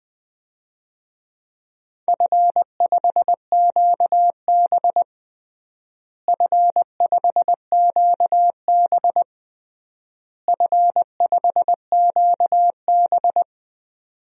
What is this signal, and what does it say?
Morse code, heard as a keyed tone, about 21 wpm, 700 Hz. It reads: F5QB F5QB F5QB
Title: DXCC-100 Call Signs - 4 Characters - Repeated 3x - 20wpm